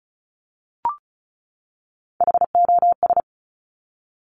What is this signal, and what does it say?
Morse code, heard as a keyed tone, about 33 wpm, 690 Hz. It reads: HOS
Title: Top Three Letter Combinations in Words 35wpm